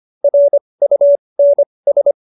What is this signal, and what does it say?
Morse code, heard as a keyed tone, about 25 wpm, spaced as for 20 wpm, 570 Hz. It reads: RUNS
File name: QSO301 20